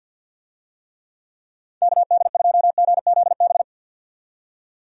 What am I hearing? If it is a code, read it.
Morse code, heard as a keyed tone, about 48 wpm, 700 Hz. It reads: KD2GZB
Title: USA Call Signs - 6 Characters - 50wpm